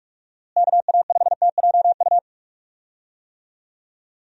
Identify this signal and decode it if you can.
Morse code, heard as a keyed tone, about 46 wpm, 700 Hz. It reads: KA5TJU
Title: USA Call Signs - 6 Characters - Repeated 2x - 45wpm